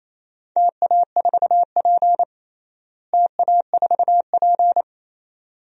Morse code, heard as a keyed tone, about 28 wpm, 700 Hz. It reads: TA4P TA4P
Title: DXCC-100 Call Signs - 4 Characters - Repeated 2x - 28wpm